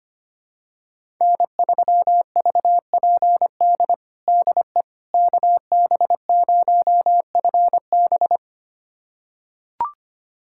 Morse code, heard as a keyed tone, about 25 wpm, 700 Hz. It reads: N3VPD DE KB0FB
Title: USA Callsign DE Callsign - 5 Characters - 25wpm